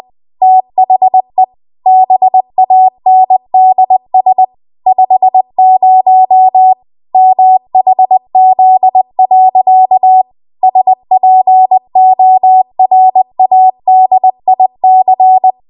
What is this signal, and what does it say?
Morse code, heard as a keyed tone, about 20 wpm, 750 Hz. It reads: THE BANDS 50 MHZ. SPORADIC
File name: Archive / 20 WPM / 20WPM